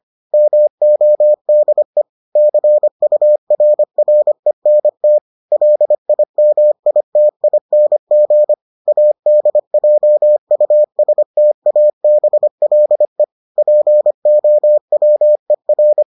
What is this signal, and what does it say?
Morse code, heard as a keyed tone, about 25 wpm, 600 Hz. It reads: MODE CURRENT LIMITING ADJUSTABLE POWER